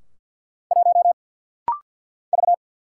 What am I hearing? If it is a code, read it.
Morse code, heard as a keyed tone, about 48 wpm, 690 Hz. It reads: 1 V
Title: Single Letter-Number - Rapid-Fire 50wpm